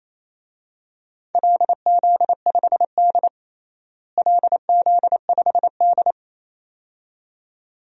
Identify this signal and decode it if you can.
Morse code, heard as a keyed tone, about 28 wpm, 700 Hz. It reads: LZ5D LZ5D
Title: DXCC-100 Call Signs - 4 Characters - Repeated 2x - 28wpm